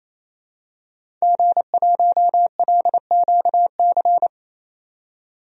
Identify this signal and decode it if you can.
Morse code, heard as a keyed tone, about 28 wpm, 700 Hz. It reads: G1LQC
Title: DXCC-100 Call Signs - 5 Characters - 28wpm